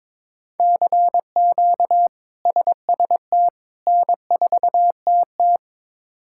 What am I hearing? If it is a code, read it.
Morse code, heard as a keyed tone, about 21 wpm, 700 Hz. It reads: CQ SST N4TT
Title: K1USN SST QSO Practice - 22wpm